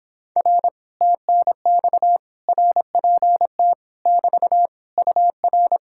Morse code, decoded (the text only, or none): R TNX RPT <BT> UR